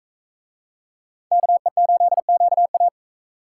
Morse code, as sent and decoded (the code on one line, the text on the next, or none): -.- . ---.. --.- .-
KE8QA